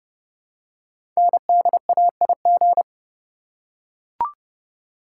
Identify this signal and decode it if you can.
Morse code, heard as a keyed tone, about 28 wpm, 700 Hz. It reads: NDAIG
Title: Code Groups - 5 Letters - 30wpm